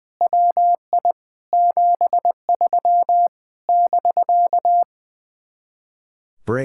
Morse code, heard as a keyed tone, about 19 wpm, 700 Hz. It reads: WI 73 <BK>
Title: Speed-Racing Common QSO Phrases Encore - 20wpm